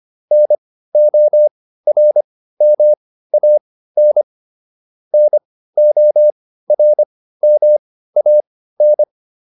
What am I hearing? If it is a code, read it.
Morse code, heard as a keyed tone, about 25 wpm, spaced as for 15 wpm, 600 Hz. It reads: NORMAN NORMAN